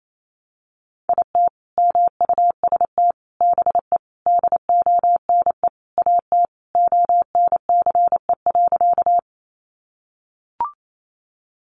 Morse code, heard as a keyed tone, about 28 wpm, 700 Hz. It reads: IT MUST BE DONE AT ONCE.
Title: Sentences from Top 500 Words 28wpm